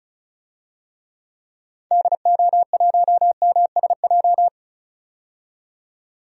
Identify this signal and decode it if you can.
Morse code, heard as a keyed tone, about 36 wpm, 690 Hz. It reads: DO1MSJ
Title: DXCC-100 Call Signs - 6 Characters - 35wpm